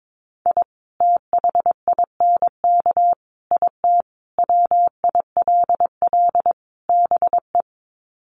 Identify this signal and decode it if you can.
Morse code, heard as a keyed tone, about 22 wpm, 700 Hz. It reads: I THINK IT WILL BE
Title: Sets of 5 Words Encore2 22wpm